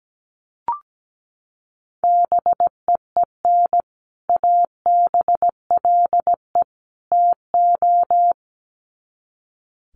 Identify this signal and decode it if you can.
Morse code, heard as a keyed tone, about 17 wpm, 700 Hz. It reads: BEEN ABLE TO
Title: Sets of 3 Words 17wpm